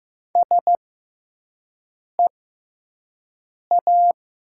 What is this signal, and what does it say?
Morse code, heard as a keyed tone, about 15 wpm, spaced as for 5 wpm, 700 Hz. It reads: SEA
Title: 6x Character Spacing - Top 300 Words - 15wpm